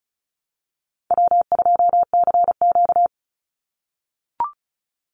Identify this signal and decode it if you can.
Morse code, heard as a keyed tone, about 34 wpm, 700 Hz. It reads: W2CQ
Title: Call Signs - 35wpm